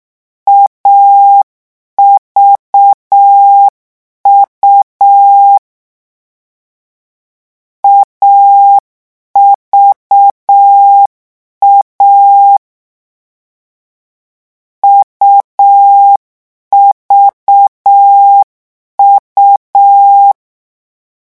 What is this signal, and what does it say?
Morse code, heard as a keyed tone, about 6 wpm, 800 Hz. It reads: AVU AVA UVU
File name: morse mp3 / lecon3-vitesse 6